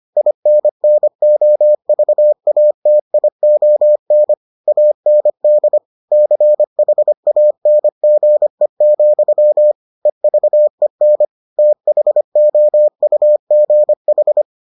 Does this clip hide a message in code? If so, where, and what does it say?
Yes: Morse code from 0.2 to 14.4 s, heard as a keyed tone, about 25 wpm, 600 Hz. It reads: INNOVATION AND CHANGE, EVEN THOUGH